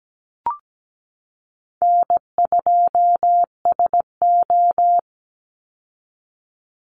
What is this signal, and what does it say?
Morse code, heard as a keyed tone, about 17 wpm, 700 Hz. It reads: N2SO
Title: Easy Call Signs Repeated 2x - 17wpm